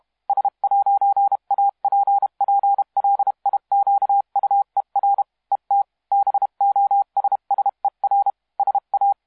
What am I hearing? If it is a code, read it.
Morse code, heard as a keyed tone, about 32 wpm, 800 Hz. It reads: S'APPLIQUER ET BOSSER SA
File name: musiqueradio / cw hsc demo f6ddr